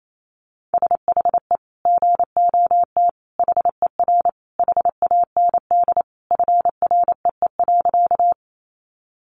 Morse code, heard as a keyed tone, about 28 wpm, 700 Hz. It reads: SHE GOT HER HAND FREE.
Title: Sentences from Top 500 Words 28wpm